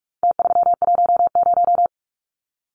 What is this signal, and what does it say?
Morse code, heard as a keyed tone, about 44 wpm, 700 Hz. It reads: T310
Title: Numbers - Contest Style - 4-Digit 45wpm